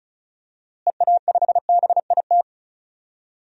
Morse code, heard as a keyed tone, about 34 wpm, 700 Hz. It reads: EA5BIT
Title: DXCC-100 Call Signs - 6 Characters - Repeated 2x - 35wpm